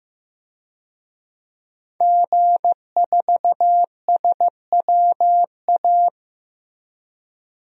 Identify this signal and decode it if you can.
Morse code, heard as a keyed tone, about 15 wpm, 700 Hz. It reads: G4SWA
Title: DXCC-100 Call Signs - 5 Characters - Repeated 3x - 15wpm